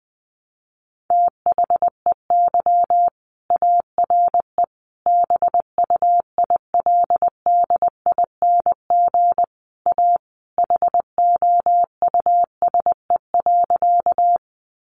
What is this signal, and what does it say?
Morse code, heard as a keyed tone, about 20 wpm, 700 Hz. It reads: THEY ARE BUILDING A HOUSE.
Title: Sentences from Top 700 Words 20wpm